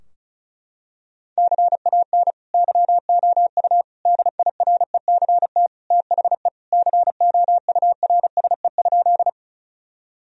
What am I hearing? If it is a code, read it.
Morse code, heard as a keyed tone, about 36 wpm, 700 Hz. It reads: CAN YOU DIRECT THE COURSE?
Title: Sentences from Top 1000 Words - Review Part 2 - 35wpm